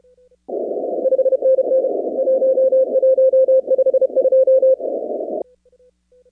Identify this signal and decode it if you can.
Morse code, heard as a keyed tone, about 32 wpm, 520 Hz. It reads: 5NN 1152
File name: contests / 2019-cw-wpx-kt5j-k5pi- -2330 0-23